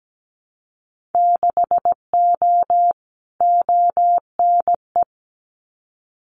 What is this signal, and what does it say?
Morse code, heard as a keyed tone, about 17 wpm, 700 Hz. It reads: 6O ONE
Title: Winter Field Day Exchanges Encore 17wpm